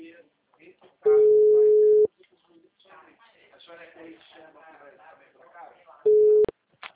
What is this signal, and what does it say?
Ringback tone (Europe, 425 Hz), with speech.